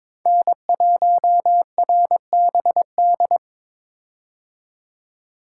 Morse code, heard as a keyed tone, about 22 wpm, 700 Hz. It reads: N1RBD
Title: K1USN SST QSO Practice - 22wpm